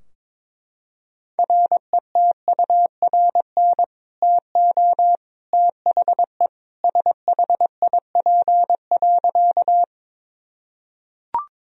Morse code, heard as a keyed tone, about 22 wpm, 700 Hz. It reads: RETURN TO THE SHIP.
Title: Sentences from Top 1000 Words - Review Part 2 - 22wpm